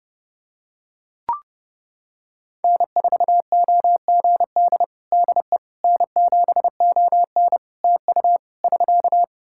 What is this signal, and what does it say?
Morse code, heard as a keyed tone, about 28 wpm, 700 Hz. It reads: N4OGD DE N7ON TU <SK>
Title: Practice QSO - Exchange 4 - Ending - 30wpm